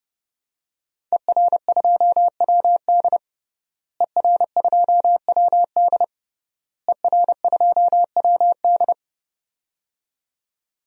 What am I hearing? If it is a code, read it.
Morse code, heard as a keyed tone, about 30 wpm, 700 Hz. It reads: ER2WD ER2WD ER2WD
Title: DXCC-100 Call Signs - 5 Characters - Repeated 3x - 30wpm